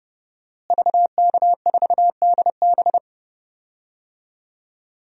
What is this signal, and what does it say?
Morse code, heard as a keyed tone, about 30 wpm, 700 Hz. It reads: VK4DB
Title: DXCC-100 Call Signs - 5 Characters - Repeated 3x - 30wpm